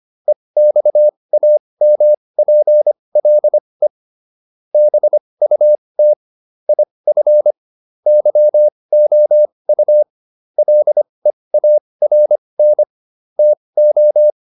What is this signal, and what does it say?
Morse code, heard as a keyed tone, about 25 wpm, spaced as for 20 wpm, 600 Hz. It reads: EXAMPLE BUT IF YOU LEARN TO